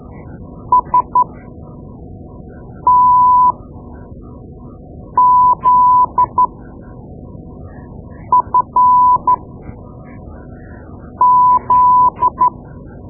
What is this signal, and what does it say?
Morse code, heard as a keyed tone, about 13 wpm, spaced as for 4 wpm, 1000 Hz. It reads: STZFZ
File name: Morse Lessons / Morse Lessons / Tape-2B